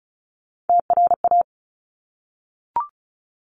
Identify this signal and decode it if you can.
Morse code, heard as a keyed tone, about 36 wpm, 700 Hz. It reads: TRA
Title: Top Three Letter Combinations in Words 35wpm